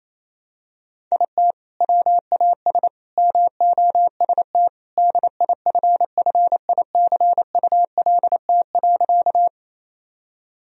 Morse code, heard as a keyed tone, about 28 wpm, 700 Hz. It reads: IT WAS MOST DIFFICULT.
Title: Sentences from Top 1000 Words - Review Part 1 - 28wpm